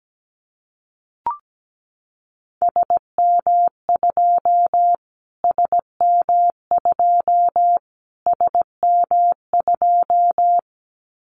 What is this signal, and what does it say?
Morse code, heard as a keyed tone, about 17 wpm, 700 Hz. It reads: SM2 SM2 SM2
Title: Callsign Prefixes for Top 25 DXCC - Repeated 3x - 17wpm